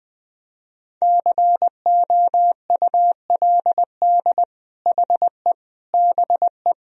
Morse code, heard as a keyed tone, about 20 wpm, 700 Hz. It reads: COULD HE BE
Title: Sentences from Top 1000 Words - Review Part 4 - 20wpm